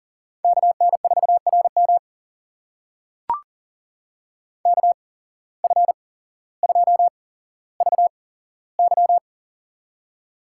Morse code, heard as a keyed tone, about 40 wpm, spaced as for 17 wpm, 700 Hz. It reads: KN4RM KF2VY
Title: USA Call Signs - 8x Character Spacing - 40wpm